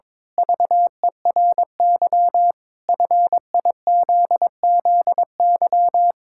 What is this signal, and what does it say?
Morse code, heard as a keyed tone, about 22 wpm, 700 Hz. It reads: VERY FIZZY
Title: Pangram Challenge v2 - 22wpm